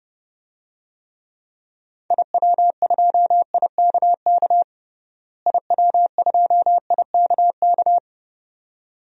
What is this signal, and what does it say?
Morse code, heard as a keyed tone, about 30 wpm, 700 Hz. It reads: IW2IKK IW2IKK
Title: DXCC-100 Call Signs - 6 Characters - Repeated 2x - 30wpm